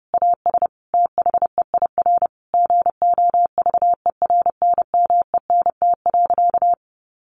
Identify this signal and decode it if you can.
Morse code, heard as a keyed tone, about 28 wpm, 700 Hz. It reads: AS THEIR GOVERNMENT.
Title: Sentences from Top 1500 Words 30wpm